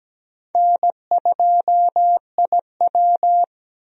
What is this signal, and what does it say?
Morse code, heard as a keyed tone, about 17 wpm, 700 Hz. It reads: N2IW
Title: USA Call Signs - 4 Characters - 17wpm